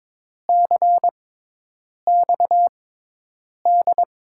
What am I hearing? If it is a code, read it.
Morse code, heard as a keyed tone, about 23 wpm, spaced as for 8 wpm, 700 Hz. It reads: CXD
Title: USA Call Signs - 6 Characters - 6x Character Spacing - 22wpm